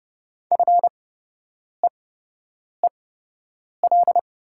Morse code, heard as a keyed tone, about 30 wpm, spaced as for 8 wpm, 700 Hz. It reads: FEEL
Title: 8x Character Spacing - Top 500 Words - 30wpm